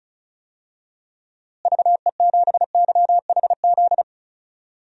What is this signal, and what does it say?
Morse code, heard as a keyed tone, about 36 wpm, 700 Hz. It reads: VE7YHZ
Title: DXCC-100 Call Signs - 6 Characters - Repeated 3x - 35wpm